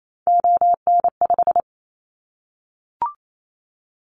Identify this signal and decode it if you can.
Morse code, heard as a keyed tone, about 28 wpm, 700 Hz. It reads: ON5
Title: Callsign Prefixes for Top 25 DXCC - Repeated 3x - 28wpm